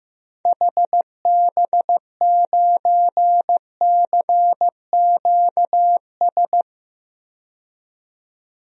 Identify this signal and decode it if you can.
Morse code, heard as a keyed tone, about 16 wpm, 700 Hz. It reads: HB9CQS
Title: DXCC-100 Call Signs - 6 Characters - 15wpm